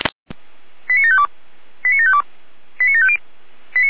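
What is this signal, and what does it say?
Five-tone selective call (ZVEI), heard as quick runs of five tones, tones of about 70 ms. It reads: 89752 89752 89750
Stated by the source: ZVEI five-tone selective call received by SDR.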